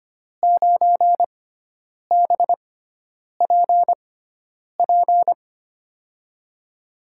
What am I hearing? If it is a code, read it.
Morse code, heard as a keyed tone, about 25 wpm, spaced as for 9 wpm, 700 Hz. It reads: 9BPP
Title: USA Call Signs - 6 Characters - 6x Character Spacing - 25wpm